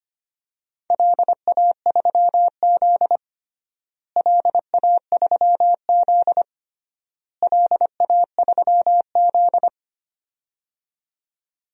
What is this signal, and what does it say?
Morse code, heard as a keyed tone, about 25 wpm, 700 Hz. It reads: LA3Z LA3Z LA3Z
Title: DXCC-100 Call Signs - 4 Characters - Repeated 3x - 25wpm